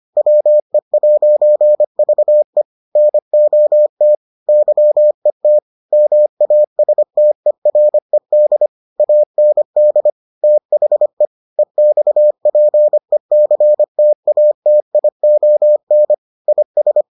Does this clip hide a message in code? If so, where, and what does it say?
Yes: Morse code from 0.2 to 17.0 s, heard as a keyed tone, about 25 wpm, 600 Hz. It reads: WE'VE NOT YET MASTERED AND THE EXPECTATION IS